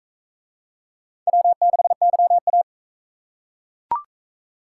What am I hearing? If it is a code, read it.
Morse code, heard as a keyed tone, about 40 wpm, 700 Hz. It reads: W6YA